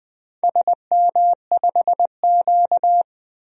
Morse code, heard as a keyed tone, about 20 wpm, 700 Hz. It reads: SM5Q
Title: DXCC-100 Call Signs - 4 Characters - 20wpm